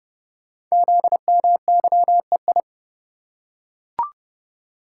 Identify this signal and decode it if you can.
Morse code, heard as a keyed tone, about 30 wpm, 700 Hz. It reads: ZMYEI
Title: Code Groups - 5 Letters - 30wpm